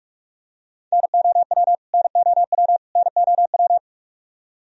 Morse code, heard as a keyed tone, about 48 wpm, 700 Hz. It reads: NOW NOW NOW
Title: Top 100 Words Repeated 45wpm